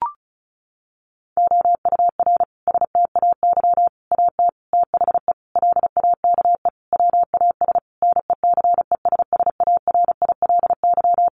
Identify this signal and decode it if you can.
Morse code, heard as a keyed tone, about 34 wpm, 700 Hz. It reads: OUR STAY AT THE LAKE WAS NECESSARILY
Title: Sentences from Top 1400 Words 35wpm